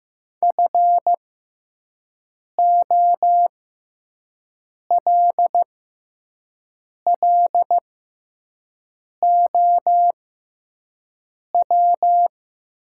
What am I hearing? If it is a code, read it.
Morse code, heard as a keyed tone, about 15 wpm, spaced as for 5 wpm, 700 Hz. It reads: FOLLOW
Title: 6x Character Spacing - Top 500 Words - 15wpm